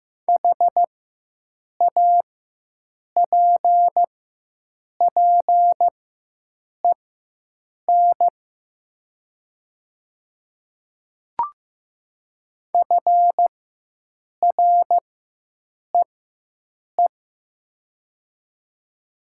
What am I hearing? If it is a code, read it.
Morse code, heard as a keyed tone, about 16 wpm, spaced as for 7 wpm, 700 Hz. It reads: HAPPEN FREE
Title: 4x Character Spacing - Top 500 Words - 15wpm